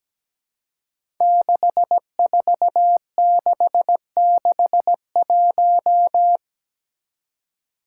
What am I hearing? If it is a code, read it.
Morse code, heard as a keyed tone, about 17 wpm, 700 Hz. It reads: 64661
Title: Five Challenging Letter-Number Combinations 17wpm